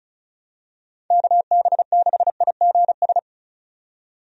Morse code, heard as a keyed tone, about 34 wpm, 700 Hz. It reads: KB6IGS
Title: USA Call Signs - 6 Characters - 35wpm